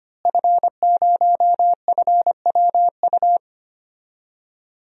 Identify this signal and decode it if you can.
Morse code, heard as a keyed tone, about 25 wpm, 700 Hz. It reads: F0FWU
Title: DXCC-100 Call Signs - 5 Characters - Repeated 2x - 25wpm